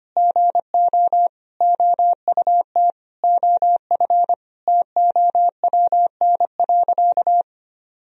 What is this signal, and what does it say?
Morse code, heard as a keyed tone, about 25 wpm, 700 Hz. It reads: GO OUT OF TOWN.
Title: Sentences from Top 1000 Words - Review Part 4 - 25wpm